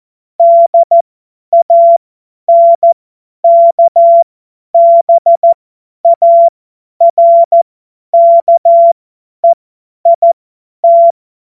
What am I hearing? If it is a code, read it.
Morse code, heard as a keyed tone, about 14 wpm, spaced as for 10 wpm, 680 Hz. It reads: DANKBARKEIT
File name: Morsekurs / Witze / Witz 20. Dezember 2022 50 F70